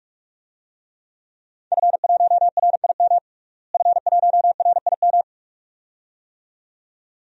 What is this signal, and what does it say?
Morse code, heard as a keyed tone, about 46 wpm, 700 Hz. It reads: F1RIM F1RIM
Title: DXCC-100 Call Signs - 5 Characters - Repeated 2x - 45wpm